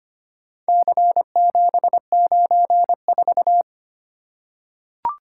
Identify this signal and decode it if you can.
Morse code, heard as a keyed tone, about 25 wpm, 700 Hz. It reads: C794
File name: practice / 25wpm / Four Letter-Number Combinations - 25wpm